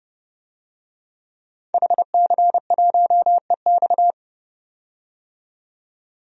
Morse code, heard as a keyed tone, about 31 wpm, 700 Hz. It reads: HC1EX
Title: DXCC-100 Call Signs - 5 Characters - 30wpm